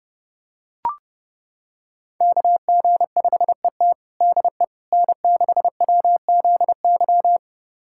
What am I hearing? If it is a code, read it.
Morse code, heard as a keyed tone, about 32 wpm, 700 Hz. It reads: KG5ET DE N6WZY
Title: USA Callsign DE Callsign - 5 Characters - 30wpm